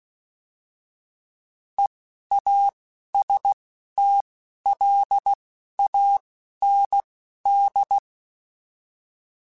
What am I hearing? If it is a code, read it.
Morse code, heard as a keyed tone, about 16 wpm, spaced as for 11 wpm, 800 Hz. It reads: EASTLAND
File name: fromSMs / FromSMs mp3s / Cities@10